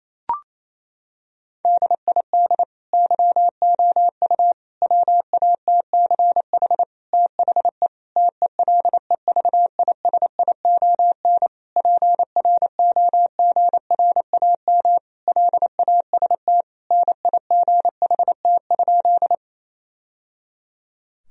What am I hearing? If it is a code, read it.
Morse code, heard as a keyed tone, about 28 wpm, 700 Hz. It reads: DID YOU WATCH THE TELEVISION PROGRAM LAST NIGHT?
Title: Sentences from Top 1000 Words - Review Part 4 - 28wpm